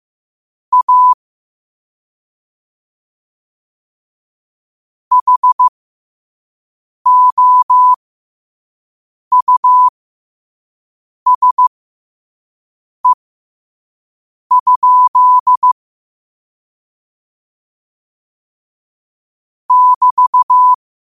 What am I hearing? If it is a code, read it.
Morse code, heard as a keyed tone, about 15 wpm, spaced as for 5 wpm, 1000 Hz. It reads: A HOUSE? <BT>
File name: qotdfiles / 5wpm